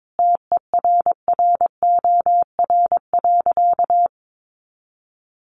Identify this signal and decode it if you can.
Morse code, heard as a keyed tone, about 22 wpm, 700 Hz. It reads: TERROR.
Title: A Sound Of Thunder 22wpm